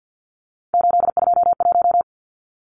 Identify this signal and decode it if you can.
Morse code, heard as a keyed tone, about 50 wpm, 700 Hz. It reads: 821
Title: Numbers - 3-Digit 50wpm